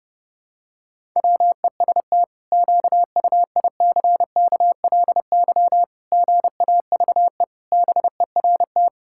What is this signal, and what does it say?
Morse code, heard as a keyed tone, about 30 wpm, 700 Hz. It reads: WEST QUICKLY GAVE BERT
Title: Pangram Challenge v2 - 30wpm